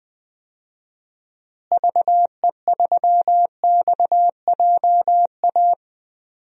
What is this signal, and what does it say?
Morse code, heard as a keyed tone, about 20 wpm, 700 Hz. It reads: VE3XJA
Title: DXCC-100 Call Signs - 6 Characters - Repeated 2x - 20wpm